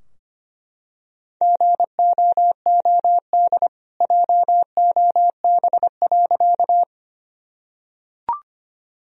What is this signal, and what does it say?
Morse code, heard as a keyed tone, about 25 wpm, 700 Hz. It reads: GOOD JOB.
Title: Sentences from Top 1000 Words - Review Part 3 - 25wpm